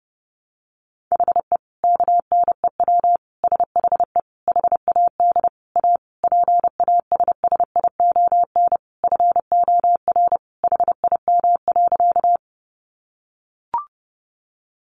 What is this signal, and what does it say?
Morse code, heard as a keyed tone, about 30 wpm, 700 Hz. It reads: HE KNEW SHE HAD A PASSION FOR HIM.
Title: Sentences from Top 1300 Words 30wpm